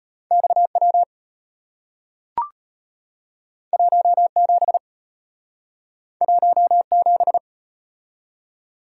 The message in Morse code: XW 17 17

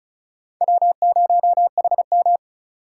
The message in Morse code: W0HM

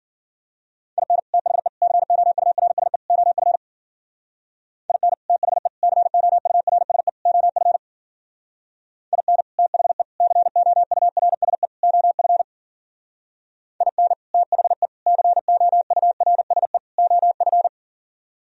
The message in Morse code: IN THE COURSE OF IN THE COURSE OF IN THE COURSE OF IN THE COURSE OF

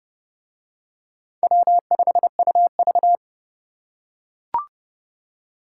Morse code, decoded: W5UV

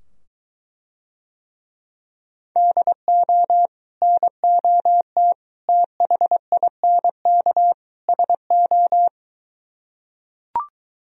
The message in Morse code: DO NOT THINK SO

